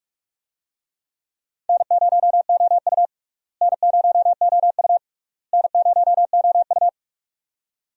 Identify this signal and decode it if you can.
Morse code, heard as a keyed tone, about 46 wpm, 700 Hz. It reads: N0OU N0OU N0OU
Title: USA Call Signs - 4 Characters - Repeated 3x - 45wpm